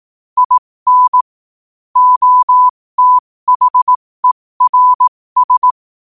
Morse code, heard as a keyed tone, about 18 wpm, spaced as for 15 wpm, 1000 Hz. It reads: IN OTHERS
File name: qotdfiles / 15wpm